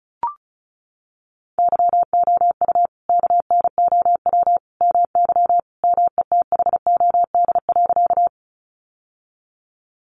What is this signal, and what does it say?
Morse code, heard as a keyed tone, about 34 wpm, 700 Hz. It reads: YOU KNOW MY METHOD.
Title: Sentences from Top 600 Words 35wpm